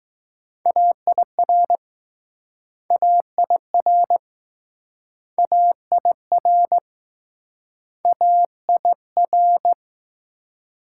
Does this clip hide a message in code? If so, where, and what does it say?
Yes: Morse code from 0.7 to 9.7 s, heard as a keyed tone, about 20 wpm, 700 Hz. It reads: AIR AIR AIR A I R